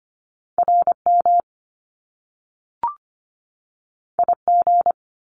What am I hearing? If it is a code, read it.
Morse code, heard as a keyed tone, about 25 wpm, 700 Hz. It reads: RM IG
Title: Top Two Letter Combinations in Words 25wpm